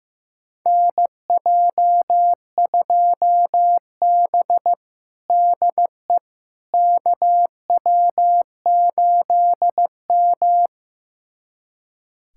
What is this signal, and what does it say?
Morse code, heard as a keyed tone, about 15 wpm, 700 Hz. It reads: NJ2B DE KW8M
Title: USA Callsign DE Callsign - 4 Characters - 15wpm